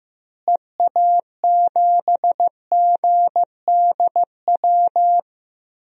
Morse code, heard as a keyed tone, about 15 wpm, 700 Hz. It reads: EA7GDW
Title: DXCC-100 Call Signs - 6 Characters - Repeated 2x - 15wpm